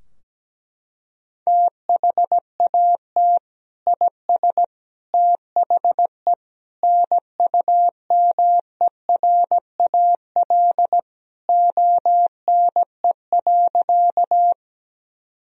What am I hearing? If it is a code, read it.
Morse code, heard as a keyed tone, about 17 wpm, 700 Hz. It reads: THAT IS THE NUMERAL ONE.
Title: Sentences from Top 1000 Words - Review Part 2 - 17wpm